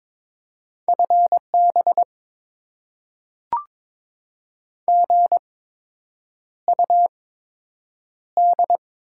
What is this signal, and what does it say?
Morse code, heard as a keyed tone, about 22 wpm, spaced as for 8 wpm, 700 Hz. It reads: FB GUD
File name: Top 20 QSO Elements - 8x Character Spacing - 22wpm